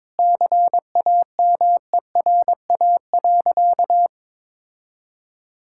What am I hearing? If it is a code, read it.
Morse code, heard as a keyed tone, about 22 wpm, 700 Hz. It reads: CAMERA.
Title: Sentences from Top 1000 Words - Review Part 5 - 22wpm